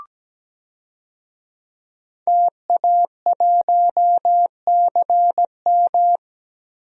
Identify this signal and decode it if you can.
Morse code, heard as a keyed tone, about 17 wpm, 700 Hz. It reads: TA1CM
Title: DXCC-100 Call Signs - 5 Characters - Repeated 2x - 17wpm